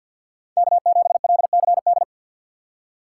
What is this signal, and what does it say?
Morse code, heard as a keyed tone, about 62 wpm, 700 Hz. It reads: K7LCD
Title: Speed Racing - USA 1x3 Call Signs 50wpm